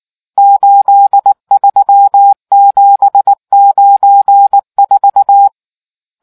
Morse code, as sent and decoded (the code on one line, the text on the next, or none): ---.. ...-- --... ----. ....-
83794